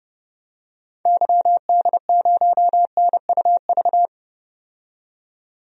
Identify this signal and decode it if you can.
Morse code, heard as a keyed tone, about 30 wpm, 700 Hz. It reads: YD0NUV
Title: DXCC-100 Call Signs - 6 Characters - 30wpm